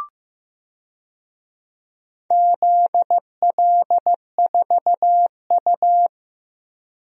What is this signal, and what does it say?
Morse code, heard as a keyed tone, about 15 wpm, 700 Hz. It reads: ZL4U
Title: DXCC-100 Call Signs - 4 Characters - 15wpm